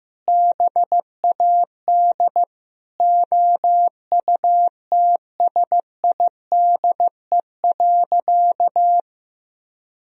Morse code, heard as a keyed tone, about 14 wpm, 700 Hz. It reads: BAD OUTSIDE.
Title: Sentences from Top 1000 Words - Review Part 4 - 15wpm